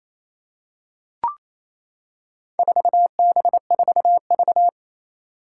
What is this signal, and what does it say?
Morse code, heard as a keyed tone, about 27 wpm, 700 Hz. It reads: 4B4V
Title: Four Challenging Letter-Number Combinations 28wpm